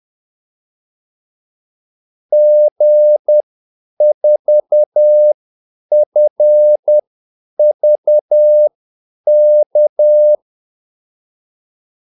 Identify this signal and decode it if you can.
Morse code, heard as a keyed tone, about 10 wpm, spaced as for 8 wpm, 600 Hz. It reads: G4FVK
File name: Callsigns-10-8wpm